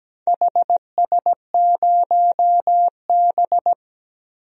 Morse code, heard as a keyed tone, about 17 wpm, 700 Hz. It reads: HS0B
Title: DXCC-100 Call Signs - 4 Characters - 17wpm